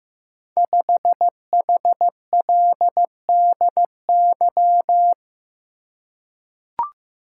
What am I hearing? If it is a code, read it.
Morse code, heard as a keyed tone, about 15 wpm, 700 Hz. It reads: 5HLDY
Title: Code Groups - 5 Letter:Numbers - 15wpm